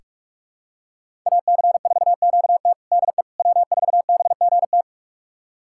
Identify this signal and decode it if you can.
Morse code, heard as a keyed tone, about 43 wpm, 700 Hz. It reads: AC4QT DE W4BGT
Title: USA Callsign DE Callsign - 5 Characters - 45wpm